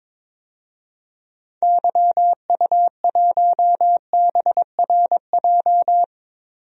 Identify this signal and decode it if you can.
Morse code, heard as a keyed tone, about 22 wpm, 700 Hz. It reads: YU1BRJ
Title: DXCC-100 Call Signs - 6 Characters - Repeated 2x - 22wpm